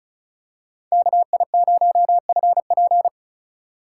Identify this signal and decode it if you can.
Morse code, heard as a keyed tone, about 36 wpm, 700 Hz. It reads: KI0FP
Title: USA Call Signs - 5 Characters - 2x Character Spacing - 35wpm